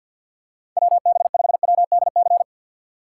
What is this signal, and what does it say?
Morse code, heard as a keyed tone, about 49 wpm, 700 Hz. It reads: WB5WDC
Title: USA Call Signs - 6 Characters - 6x Character Spacing - 50wpm